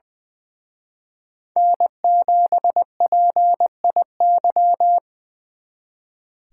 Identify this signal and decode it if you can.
Morse code, heard as a keyed tone, about 24 wpm, 700 Hz. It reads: N7PIY
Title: Speed Racing - USA 1x3 Call Signs 20wpm